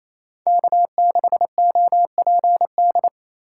Morse code, heard as a keyed tone, about 36 wpm, 700 Hz. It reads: K6OPD